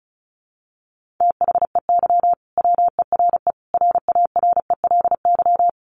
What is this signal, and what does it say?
Morse code, heard as a keyed tone, about 34 wpm, 700 Hz. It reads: THEY WERE RARELY
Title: Sentences from Top 1400 Words 35wpm